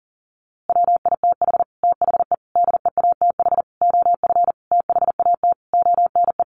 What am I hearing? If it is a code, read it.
Morse code, heard as a keyed tone, about 40 wpm, 700 Hz. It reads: WITH THE DEATH OF THAT ONE
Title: A Sound Of Thunder 40wpm